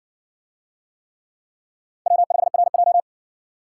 Morse code, heard as a keyed tone, about 60 wpm, 700 Hz. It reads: W5RJ